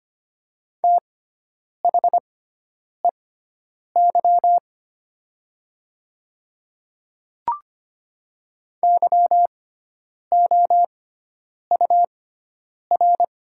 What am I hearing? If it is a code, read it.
Morse code, heard as a keyed tone, about 25 wpm, spaced as for 9 wpm, 700 Hz. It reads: THEY YOUR
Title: 6x Character Spacing - Top 200 Words - 25wpm